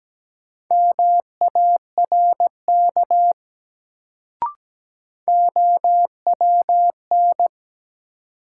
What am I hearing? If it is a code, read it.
Morse code, heard as a keyed tone, about 17 wpm, 700 Hz. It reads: MARK OWN
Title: Sentences from Top 300 Words - New Word Drill - 17wpm